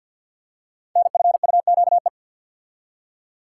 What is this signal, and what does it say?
Morse code, heard as a keyed tone, about 53 wpm, 700 Hz. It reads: NFUQE
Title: Code Groups - 5 Letter:Numbers - 50wpm